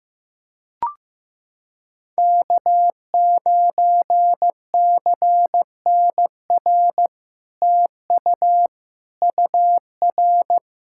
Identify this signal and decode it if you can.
Morse code, heard as a keyed tone, about 15 wpm, 700 Hz. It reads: K9CNR TU UR
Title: SOTA QSO Practice 15wpm